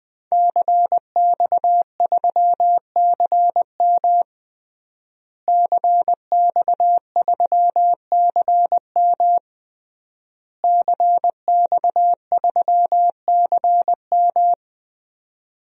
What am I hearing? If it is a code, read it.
Morse code, heard as a keyed tone, about 21 wpm, 700 Hz. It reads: CX3CM CX3CM CX3CM
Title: DXCC-100 Call Signs - 5 Characters - Repeated 3x - 20wpm